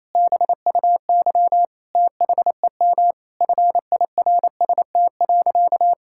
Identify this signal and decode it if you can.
Morse code, heard as a keyed tone, about 27 wpm, 700 Hz. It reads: BUY THEM FIRST.
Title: Sentences from Top 1000 Words - Review Part 6 - 28wpm